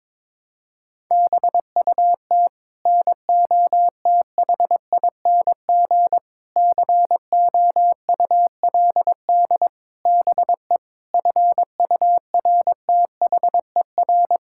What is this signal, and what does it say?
Morse code, heard as a keyed tone, about 21 wpm, 700 Hz. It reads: BUT NOTHING COULD BE FURTHER
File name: practice / 22wpm / Sets of 8 Words - 22wpm